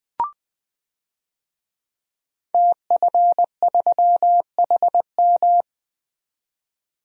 Morse code, heard as a keyed tone, about 21 wpm, 700 Hz. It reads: TF3HM
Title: DXCC-100 Call Signs - 5 Characters - 20wpm